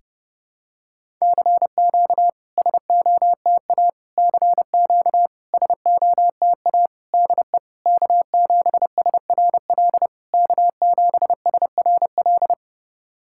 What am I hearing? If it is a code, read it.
Morse code, heard as a keyed tone, about 29 wpm, 660 Hz. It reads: CQ SOTA CQ SOTA DE K7SRL K7SRL
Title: SOTA QSO Practice 30wpm